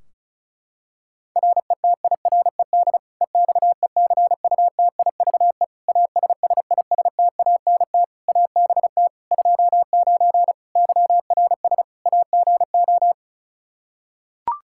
Morse code, heard as a keyed tone, about 34 wpm, 700 Hz. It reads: RETIRED EXECUTIVE ASSISTANT ABT 29 YRS AGO
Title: Practice QSO - Exchange 3 - More Info - 35wpm